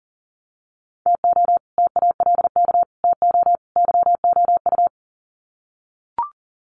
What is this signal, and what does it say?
Morse code, heard as a keyed tone, about 40 wpm, 700 Hz. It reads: TO TALK TO YOU
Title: Sets of 4 Words 40wpm